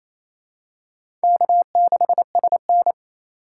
Morse code, heard as a keyed tone, about 28 wpm, 700 Hz. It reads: K6SN
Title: USA Call Signs - 4 Characters - 2x Character Spacing - 28wpm